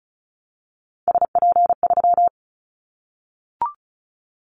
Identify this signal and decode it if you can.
Morse code, heard as a keyed tone, about 34 wpm, 700 Hz. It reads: SP3
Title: Callsign Prefixes for Top 25 DXCC - Repeated 3x - 35wpm